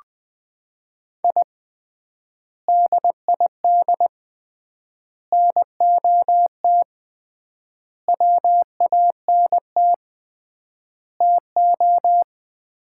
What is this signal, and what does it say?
Morse code, heard as a keyed tone, about 20 wpm, 700 Hz. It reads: I DID NOT WANT TO
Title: 3x Word-Spacing - Sets of 5 Words - 20wpm